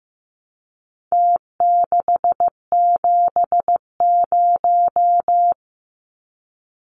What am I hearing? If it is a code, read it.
Morse code, heard as a keyed tone, about 15 wpm, 700 Hz. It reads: T670
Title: Numbers - Contest Style - 4-Digit 15wpm